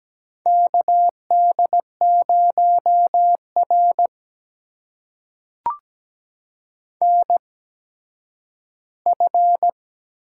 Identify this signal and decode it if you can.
Morse code, heard as a keyed tone, about 17 wpm, spaced as for 10 wpm, 700 Hz. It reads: KD0R NF
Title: USA Call Signs - 4 Characters - 8x Character Spacing - 17wpm